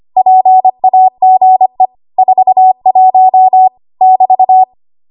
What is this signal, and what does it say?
Morse code, heard as a keyed tone, about 25 wpm, 750 Hz. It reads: PAGE 41 <BT>
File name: Archive / 25 WPM / 25WPM